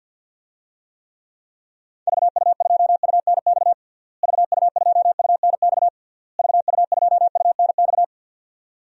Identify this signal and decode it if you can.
Morse code, heard as a keyed tone, about 47 wpm, 700 Hz. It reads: VU2UNX VU2UNX VU2UNX
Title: DXCC-100 Call Signs - 6 Characters - Repeated 3x - 50wpm